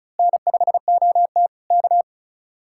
Morse code, heard as a keyed tone, about 32 wpm, 700 Hz. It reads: N5OT K
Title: Speed-Racing Common QSO Phrases - 35wpm